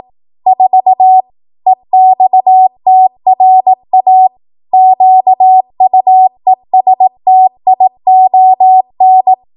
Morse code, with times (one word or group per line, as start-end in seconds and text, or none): 0.5-1.2 4
1.7-4.3 EXTRA
4.7-9.3 QUESTION